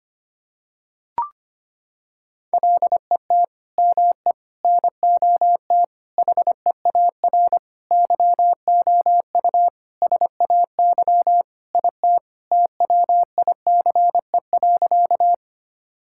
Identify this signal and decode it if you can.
Morse code, heard as a keyed tone, about 25 wpm, 700 Hz. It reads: LET ME NOT HEAR YOU SAY IT TWICE.
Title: Sentences from Top 1600 Words 25wpm